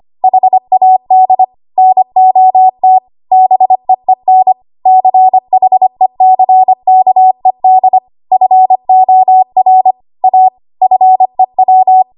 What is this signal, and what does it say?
Morse code, heard as a keyed tone, about 25 wpm, 750 Hz. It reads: HAD NOT BEEN CHECKED FOR A FEW
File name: Archive / 25 WPM / 25WPM